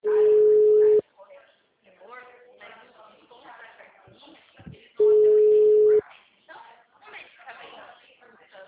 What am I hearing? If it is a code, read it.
Ringback tone (Europe, 425 Hz), with speech.
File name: out- -2010- - - .83909